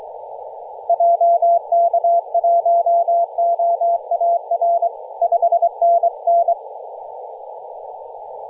Morse code, heard as a keyed tone, about 23 wpm, 680 Hz. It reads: JK1OAR 5NN
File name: posts / 2016-nov-13-0348z-t30jy-28-034mhz-cw